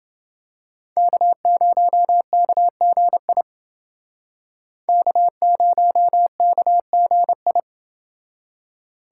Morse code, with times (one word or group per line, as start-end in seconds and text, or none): 1.0-3.4 K0KGI
4.9-7.6 K0KGI